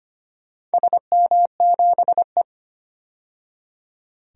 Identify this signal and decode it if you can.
Morse code, heard as a keyed tone, about 25 wpm, 700 Hz. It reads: SM7E
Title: DXCC-100 Call Signs - 4 Characters - 25wpm